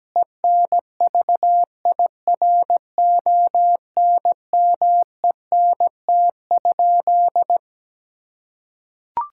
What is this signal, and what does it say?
Morse code, heard as a keyed tone, about 17 wpm, 700 Hz. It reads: ENVIRONMENT?
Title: Sentences from Top 1000 Words - Review Part 5 - 17wpm